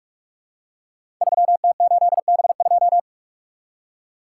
Morse code, heard as a keyed tone, about 45 wpm, 700 Hz. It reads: 3T8B2
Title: Code Groups - 5 Letter:Numbers - 45wpm